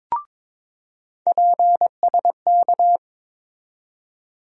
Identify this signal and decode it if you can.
Morse code, heard as a keyed tone, about 22 wpm, 700 Hz. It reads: PSK
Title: USA Call Sign Postfixes - 3 Letters 22wpm